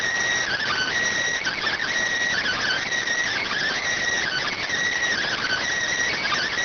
Emergency Alert System (SAME) tones (header bursts), heard as harsh buzzing bursts. It is heard from 0.6 to 6.1 s.